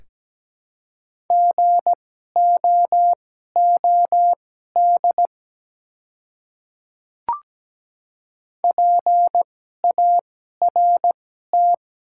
Morse code, heard as a keyed tone, about 17 wpm, spaced as for 12 wpm, 700 Hz. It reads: GOOD PART
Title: 2x Character Spacing - Top 300 Words - 17wpm